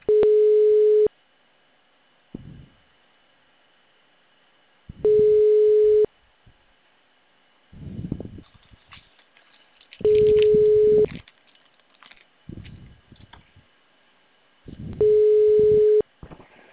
Ringback tone (Europe, 425 Hz).